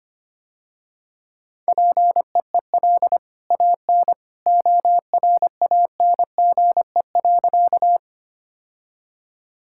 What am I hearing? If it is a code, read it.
Morse code, heard as a keyed tone, about 25 wpm, 700 Hz. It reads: PEEL AN ORANGE.
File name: Easy Sentences with Hints - 25wpm